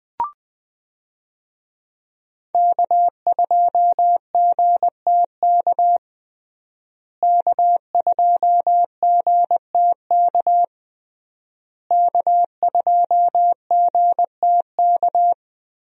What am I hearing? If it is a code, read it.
Morse code, heard as a keyed tone, about 21 wpm, 700 Hz. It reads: K2GTK K2GTK K2GTK
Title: USA Call Signs - 5 Characters - Repeated 3x - 20wpm